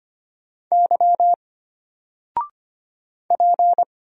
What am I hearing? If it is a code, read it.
Morse code, heard as a keyed tone, about 25 wpm, 700 Hz. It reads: Y P
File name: Single-Letter-Number-25wpm